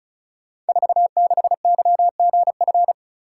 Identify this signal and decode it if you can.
Morse code, heard as a keyed tone, about 36 wpm, 700 Hz. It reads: 46YGF
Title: Code Groups - 5 Letter:Numbers - 35wpm